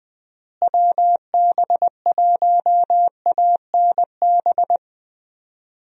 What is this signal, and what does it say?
Morse code, heard as a keyed tone, about 20 wpm, 700 Hz. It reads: WB1ANB
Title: USA Call Signs - 6 Characters - Repeated 3x - 20wpm